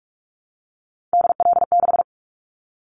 Morse code, heard as a keyed tone, about 44 wpm, 700 Hz. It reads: DL6
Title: Callsign Prefixes for Top 25 DXCC - 45wpm